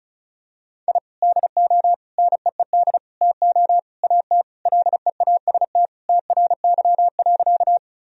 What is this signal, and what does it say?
Morse code, heard as a keyed tone, about 36 wpm, 700 Hz. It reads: I DO NEED TO AT LEAST TRY.
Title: Sentences from Top 1000 Words - Review Part 1 - 35wpm